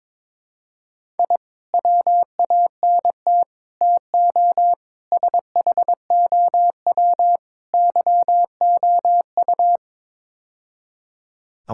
Morse code, heard as a keyed tone, about 22 wpm, 700 Hz. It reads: I WANT TO SHOW YOU
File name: practice / 22wpm / Build A Phrase - 5-Word Phrases - 22wpm